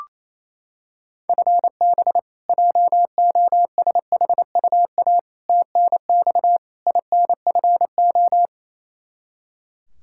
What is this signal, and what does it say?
Morse code, heard as a keyed tone, about 28 wpm, 700 Hz. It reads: FB JOSHUA TNX INFO
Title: K1USN SST QSO Practice 28wpm